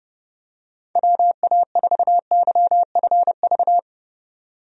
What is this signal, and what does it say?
Morse code, heard as a keyed tone, about 31 wpm, 700 Hz. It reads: WA4YFV
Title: USA Call Signs - 6 Characters - 30wpm